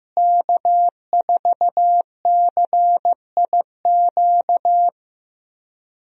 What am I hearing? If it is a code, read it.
Morse code, heard as a keyed tone, about 15 wpm, 700 Hz. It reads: K4CIQ
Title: USA Call Signs - 5 Characters - Repeated 3x - 15wpm